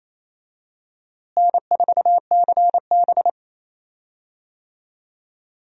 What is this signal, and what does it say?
Morse code, heard as a keyed tone, about 28 wpm, 700 Hz. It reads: N4CB
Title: USA Call Signs - 4 Characters - Repeated 3x - 28wpm